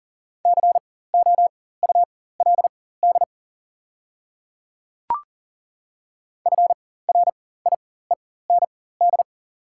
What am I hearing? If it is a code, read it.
Morse code, heard as a keyed tone, about 40 wpm, spaced as for 19 wpm, 690 Hz. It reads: COULD FRIEND
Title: 4x Character Spacing - Top 500 Words - 40wpm